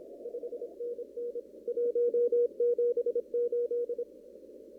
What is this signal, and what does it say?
Morse code, heard as a keyed tone, about 28 wpm, 480 Hz. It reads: 5NN 178
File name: CW / kt5j wpx cw-op-k5pi